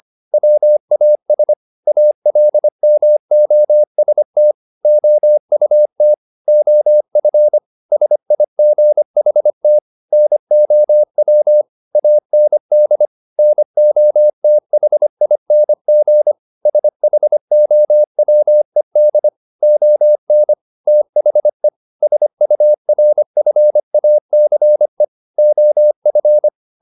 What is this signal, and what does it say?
Morse code, heard as a keyed tone, about 25 wpm, 600 Hz. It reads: WAS ALMOST OUT OF SIGHT NOW AND NOTHING SHOWED ON THE SURFACE OF
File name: the old man and the sea part 1 25wpm